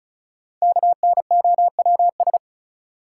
Morse code, heard as a keyed tone, about 36 wpm, 700 Hz. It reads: KNOWS
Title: Sentences from Top 1000 Words - Review Part 1 - New Word Drill - 35wpm